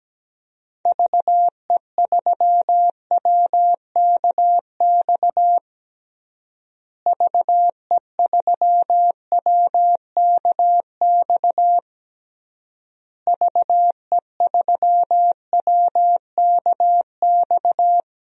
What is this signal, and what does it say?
Morse code, heard as a keyed tone, about 17 wpm, 700 Hz. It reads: VE3WKX VE3WKX VE3WKX
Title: DXCC-100 Call Signs - 6 Characters - Repeated 3x - 17wpm